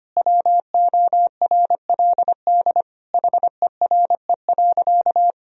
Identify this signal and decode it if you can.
Morse code, heard as a keyed tone, about 25 wpm, 700 Hz. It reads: WORLD HERE.
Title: Sentences from Top 1000 Words - Review Part 3 - 25wpm